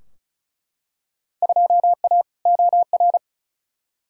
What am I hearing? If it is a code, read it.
Morse code, heard as a keyed tone, about 36 wpm, 700 Hz. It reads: 2A OR
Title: Field Day Exchanges - 35wpm